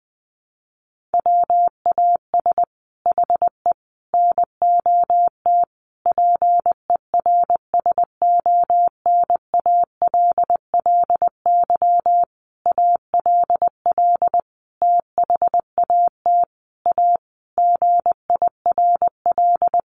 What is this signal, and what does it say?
Morse code, heard as a keyed tone, about 20 wpm, 700 Hz. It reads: WAS HE NOT PERSONALLY ALL THAT A GIRL
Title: Sentences from Top 1400 Words 20wpm